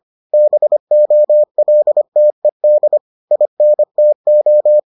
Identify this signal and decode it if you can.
Morse code, heard as a keyed tone, about 25 wpm, 600 Hz. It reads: BOLTED INTO